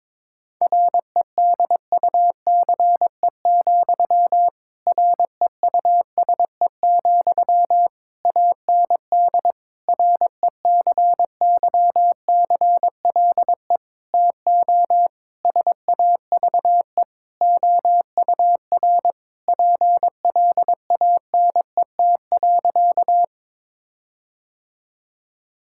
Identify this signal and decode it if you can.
Morse code, heard as a keyed tone, about 22 wpm, 700 Hz. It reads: REDUCE, REUSE, AND RECYCLE TO SAVE OUR PLANET.
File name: Difficult Sentences with Hints - 22wpm